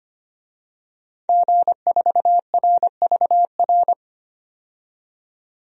Morse code, heard as a keyed tone, about 25 wpm, 700 Hz. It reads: G4RVR
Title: DXCC-100 Call Signs - 5 Characters - 25wpm